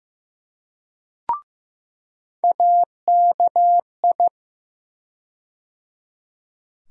Morse code, heard as a keyed tone, about 15 wpm, 700 Hz. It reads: AKI
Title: USA Call Sign Postfixes - 3 Letters 15wpm